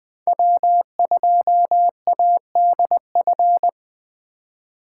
Morse code, heard as a keyed tone, about 20 wpm, 700 Hz. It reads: W2ADF
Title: Speed Racing - USA Call Signs 17wpm